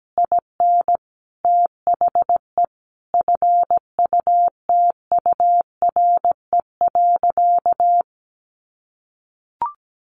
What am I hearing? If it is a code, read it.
Morse code, heard as a keyed tone, about 17 wpm, 700 Hz. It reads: IN THE FUTURE.
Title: Sentences from Top 1200 Words 17wpm